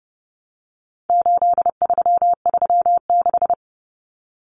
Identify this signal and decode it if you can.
Morse code, heard as a keyed tone, about 30 wpm, 700 Hz. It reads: 8336
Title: Common Cut Numbers - 4-Digit 30wpm